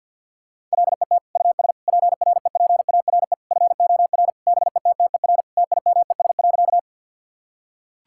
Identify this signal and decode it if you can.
Morse code, heard as a keyed tone, about 50 wpm, 700 Hz. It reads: LET US PREPARE FOR BETTER TIMES.
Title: Sentences from Top 1000 Words - Review Part 6 - 50wpm